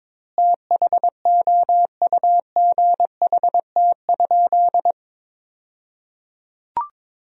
Morse code, heard as a keyed tone, about 22 wpm, 700 Hz. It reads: THOUGHT?
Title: 2x Word-Spacing - Sentences from Top 300 Words - 22wpm